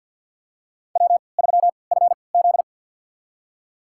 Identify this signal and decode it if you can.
Morse code, heard as a keyed tone, about 52 wpm, spaced as for 28 wpm, 700 Hz. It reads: W3FZ
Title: USA Call Signs - 4 Characters - 3x Character Spacing - 50wpm